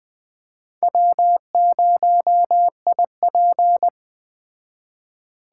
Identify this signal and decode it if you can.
Morse code, heard as a keyed tone, about 20 wpm, 700 Hz. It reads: W0IP
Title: Speed Racing - USA 1x2 Call Signs 17wpm